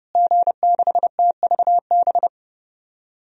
Morse code, heard as a keyed tone, about 32 wpm, 700 Hz. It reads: G6TVB
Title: DXCC-100 Call Signs - 5 Characters - Repeated 3x - 30wpm